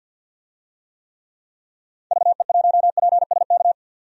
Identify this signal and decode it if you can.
Morse code, heard as a keyed tone, about 51 wpm, 700 Hz. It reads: VE1PSK
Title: DXCC-100 Call Signs - 6 Characters - 50wpm